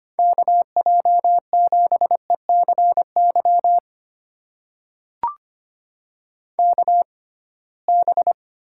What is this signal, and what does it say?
Morse code, heard as a keyed tone, about 25 wpm, spaced as for 19 wpm, 700 Hz. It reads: KJ7ECY KB
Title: USA Call Signs - 6 Characters - 6x Character Spacing - 25wpm